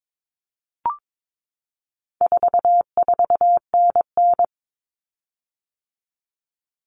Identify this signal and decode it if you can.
Morse code, heard as a keyed tone, about 22 wpm, 700 Hz. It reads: 44NN
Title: Common Cut Numbers - 4-Digit 22wpm